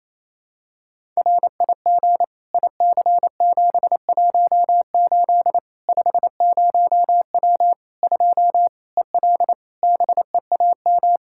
Morse code, heard as a keyed tone, about 28 wpm, 700 Hz. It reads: RIG IC718 50W 2 EL BEAM
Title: K1USN SST QSO Practice 28wpm